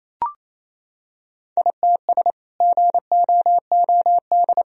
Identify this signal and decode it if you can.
Morse code, heard as a keyed tone, about 28 wpm, 700 Hz. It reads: ITS GOOD
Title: Sentences from Top 1000 Words - Review Part 4 - 28wpm